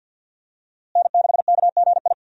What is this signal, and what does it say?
Morse code, heard as a keyed tone, about 53 wpm, 700 Hz. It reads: N6KGI
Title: USA Call Signs - 5 Characters - Repeated 2x - 50wpm